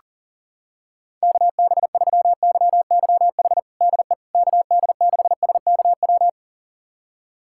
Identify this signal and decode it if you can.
Morse code, heard as a keyed tone, about 40 wpm, 700 Hz. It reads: KB3YYH DE KD6SKW
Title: USA Callsign DE Callsign - 6 Characters - 40wpm